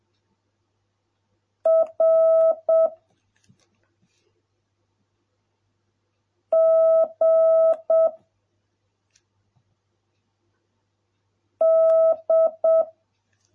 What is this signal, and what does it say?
Morse code, heard as a keyed tone, about 7 wpm, 650 Hz. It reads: R G D